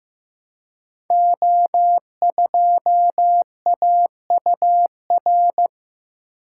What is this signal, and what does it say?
Morse code, heard as a keyed tone, about 15 wpm, 700 Hz. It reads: O2AUR
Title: Code Groups - 5 Letter:Numbers - 15wpm